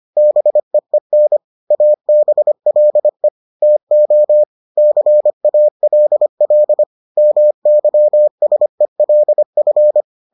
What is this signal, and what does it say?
Morse code, heard as a keyed tone, about 25 wpm, 600 Hz. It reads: BEEN ABLE TO CALL MYSELF